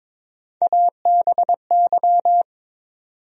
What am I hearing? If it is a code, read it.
Morse code, heard as a keyed tone, about 21 wpm, 700 Hz. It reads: ABY
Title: USA Call Sign Postfixes - 3 Letters 22wpm